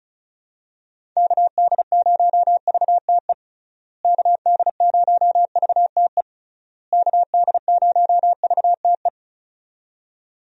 Morse code, heard as a keyed tone, about 34 wpm, 700 Hz. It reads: KD0VTE KD0VTE KD0VTE
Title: USA Call Signs - 6 Characters - Repeated 3x - 35wpm